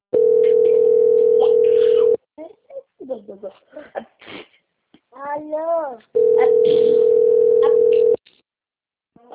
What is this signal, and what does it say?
Ringback tone (North America), 440 + 480 Hz, with speech.
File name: out- - - - - .43750